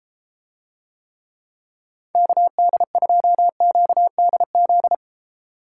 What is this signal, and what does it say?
Morse code, heard as a keyed tone, about 30 wpm, 700 Hz. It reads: KD2QDZ